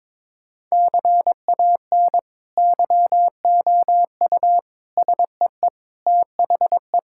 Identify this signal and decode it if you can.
Morse code, heard as a keyed tone, about 22 wpm, 700 Hz. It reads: CAN YOU SEE THE
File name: practice / 22wpm / Sentences with 2-Word Phrases - 22wpm